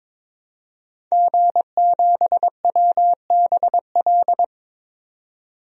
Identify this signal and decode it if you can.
Morse code, heard as a keyed tone, about 22 wpm, 700 Hz. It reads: G7WBL
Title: DXCC-100 Call Signs - 5 Characters - 22wpm